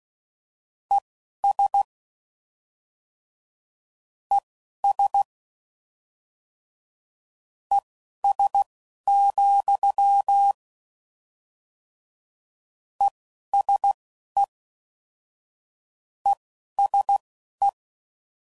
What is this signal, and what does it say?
Morse code, heard as a keyed tone, about 16 wpm, spaced as for 12 wpm, 800 Hz. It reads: ES ES ES, ESE ESE